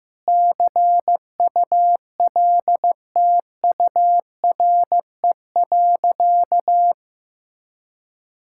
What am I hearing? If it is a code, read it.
Morse code, heard as a keyed tone, about 16 wpm, 700 Hz. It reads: CULTURE.
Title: Sentences from Top 1000 Words - Review Part 6 - 15wpm